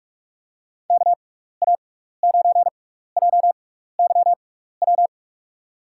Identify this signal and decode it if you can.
Morse code, heard as a keyed tone, about 45 wpm, spaced as for 15 wpm, 700 Hz. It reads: KA9JYW
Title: USA Call Signs - 6x Character Spacing - 45wpm